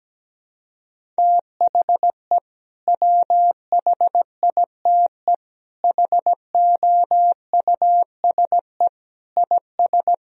Morse code, heard as a keyed tone, about 17 wpm, 700 Hz. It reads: THE WHITE HOUSE IS
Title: Sentences from Top 1000 Words - Review Part 1 - 17wpm